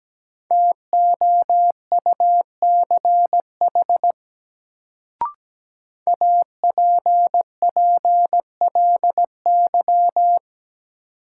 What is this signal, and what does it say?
Morse code, heard as a keyed tone, about 17 wpm, 700 Hz. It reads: TOUCH APPLY
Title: Sentences from Top 1200 Words - New Word Drill - 17wpm